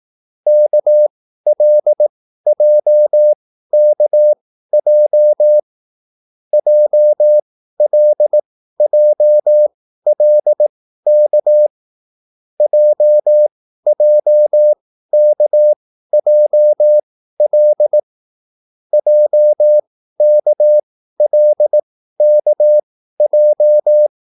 Morse code, heard as a keyed tone, about 18 wpm, spaced as for 13 wpm, 600 Hz. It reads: KLJKJ JLJLK JJKJL JKLKJ
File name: QSL / CW lesson